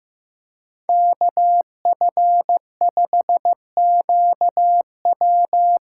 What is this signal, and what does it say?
Morse code, heard as a keyed tone, about 15 wpm, 700 Hz. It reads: KF5QW
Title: USA Call Signs - 6 Characters - 6x Character Spacing - 15wpm